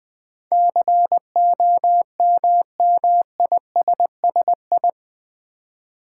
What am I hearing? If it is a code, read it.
Morse code, heard as a keyed tone, about 20 wpm, 700 Hz. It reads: COMMISSI
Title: Build A Word - 4 Letters At A Time - 12-Letter Words - 20wpm